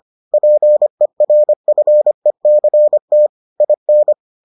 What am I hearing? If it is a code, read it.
Morse code, heard as a keyed tone, about 25 wpm, 600 Hz. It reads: PERFECT IN